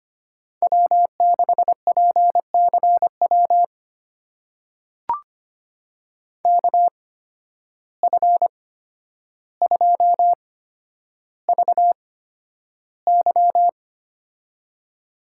Morse code, heard as a keyed tone, about 25 wpm, spaced as for 11 wpm, 700 Hz. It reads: W6PCW KF2VY
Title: USA Call Signs - 5 Characters - 8x Character Spacing - 25wpm